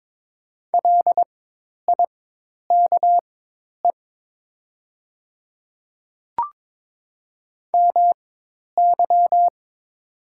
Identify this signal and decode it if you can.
Morse code, heard as a keyed tone, about 22 wpm, spaced as for 10 wpm, 700 Hz. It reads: LIKE MY
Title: 4x Character Spacing - Top 100 Words - 22wpm